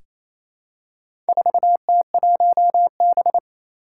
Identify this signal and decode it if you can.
Morse code, heard as a keyed tone, about 28 wpm, 700 Hz. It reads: 4T1B
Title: Four Challenging Letter-Number Combinations 28wpm